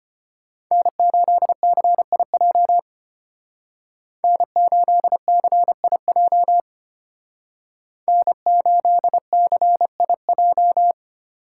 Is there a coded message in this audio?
Yes: Morse code from 0.7 to 10.9 s, heard as a keyed tone, about 32 wpm, 700 Hz. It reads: N8CIJ N8CIJ N8CIJ